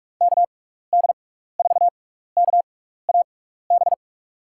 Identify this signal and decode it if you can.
Morse code, heard as a keyed tone, about 47 wpm, spaced as for 16 wpm, 700 Hz. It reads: KD4KAB
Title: USA Call Signs - 6 Characters - 6x Character Spacing - 45wpm